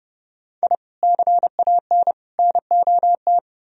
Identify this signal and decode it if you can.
Morse code, heard as a keyed tone, about 37 wpm, 700 Hz. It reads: I CAN NOT
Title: Speed-Racing - Sets of 3 Words 30wpm